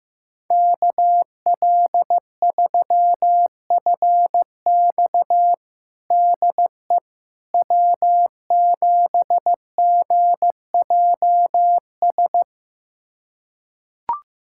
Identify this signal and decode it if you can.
Morse code, heard as a keyed tone, about 15 wpm, 700 Hz. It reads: KL3FX DE W7GJS
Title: USA Callsign DE Callsign - 5 Characters - 15wpm